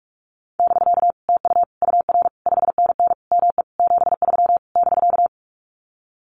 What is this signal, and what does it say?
Morse code, heard as a keyed tone, about 44 wpm, 700 Hz. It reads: <BK> TU UR 5NN ME 73 <BK>
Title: Common QSO Phrases Encore 45wpm